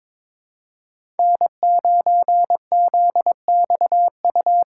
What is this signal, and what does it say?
Morse code, heard as a keyed tone, about 22 wpm, 700 Hz. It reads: N9ZXU
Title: USA Call Signs - 2x Character Spacing - 22wpm